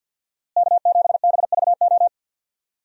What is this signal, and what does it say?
Morse code, heard as a keyed tone, about 52 wpm, 700 Hz. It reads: K7BVO
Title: USA Call Signs - 5 Characters - Repeated 2x - 50wpm